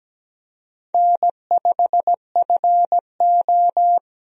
Morse code, heard as a keyed tone, about 17 wpm, 700 Hz. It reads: N5FO
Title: USA Call Signs - 4 Characters - 17wpm